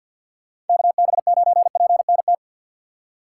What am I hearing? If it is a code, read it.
Morse code, heard as a keyed tone, about 53 wpm, 700 Hz. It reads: KB9PNT
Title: USA Call Signs - 6 Characters - 4x Character Spacing - 50wpm